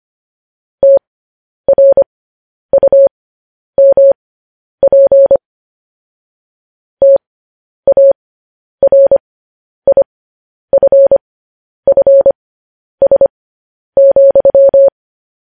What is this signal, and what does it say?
Morse code, heard as a keyed tone, about 25 wpm, spaced as for 10 wpm, 550 Hz. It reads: TRUMP TARIFFS,
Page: news-headlines-in-morse-code-at-10-wpm